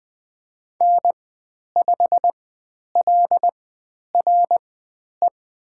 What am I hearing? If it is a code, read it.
Morse code, heard as a keyed tone, about 20 wpm, spaced as for 10 wpm, 700 Hz. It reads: N5LRE
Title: 109 Lesson 18g - Letter m - Call Signs 20wpm